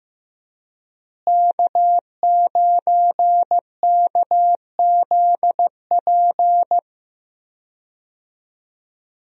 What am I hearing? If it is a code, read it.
Morse code, heard as a keyed tone, about 15 wpm, 700 Hz. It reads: K9KZP
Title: USA Call Signs - 5 Characters - Repeated 3x - 15wpm